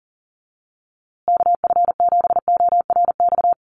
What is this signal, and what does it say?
Morse code, heard as a keyed tone, about 37 wpm, 700 Hz. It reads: KF7ORX
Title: USA Call Signs Repeated 3x 40wpm